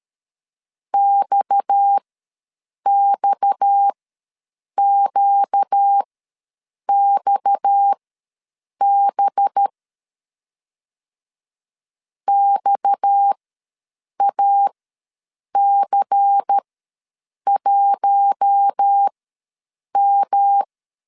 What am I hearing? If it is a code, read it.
Morse code, heard as a keyed tone, about 13 wpm, spaced as for 7 wpm, 800 Hz. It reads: XXQXB XAC1M
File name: LEZIONI / 6 PKX0 / 6 PKX ws 12wpm 3106